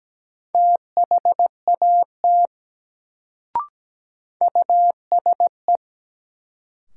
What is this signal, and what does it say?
Morse code, heard as a keyed tone, about 17 wpm, 700 Hz. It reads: THAT USE
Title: Sentences from Top 100 Words - New Word Drill - 17wpm